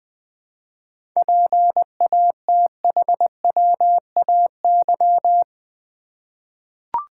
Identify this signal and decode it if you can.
Morse code, heard as a keyed tone, about 20 wpm, 700 Hz. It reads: PATHWAY
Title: 7 Letter Words - First 3 Letters Given - 20wpm